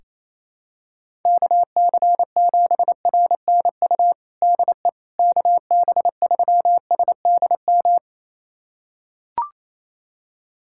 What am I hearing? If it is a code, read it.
Morse code, heard as a keyed tone, about 28 wpm, 700 Hz. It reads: KC7RNU DE KB3SDM
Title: USA Callsign DE Callsign - 6 Characters - 28wpm